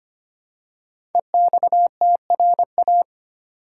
Morse code, heard as a keyed tone, about 25 wpm, 700 Hz. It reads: EXTRA
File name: 5 Letter Words - First 3 Letters Given - 25wpm